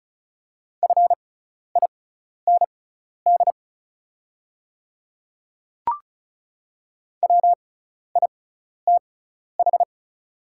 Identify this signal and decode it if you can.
Morse code, heard as a keyed tone, about 36 wpm, spaced as for 12 wpm, 690 Hz. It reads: FIND WITH
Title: 6x Character Spacing - Top 100 Words - 35wpm